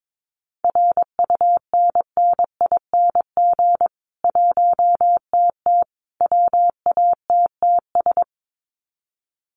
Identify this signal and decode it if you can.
Morse code, heard as a keyed tone, about 22 wpm, 700 Hz. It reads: RUNNING 1TT WATTS
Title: Common QSO Phrases 22wpm